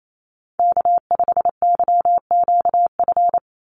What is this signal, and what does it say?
Morse code, heard as a keyed tone, about 28 wpm, 700 Hz. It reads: K5YQF
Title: Call Signs - 28wpm